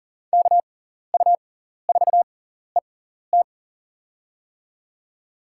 Morse code, heard as a keyed tone, about 38 wpm, spaced as for 14 wpm, 700 Hz. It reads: KU4ET
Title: USA Call Signs - 5 Characters - 6x Character Spacing - 40wpm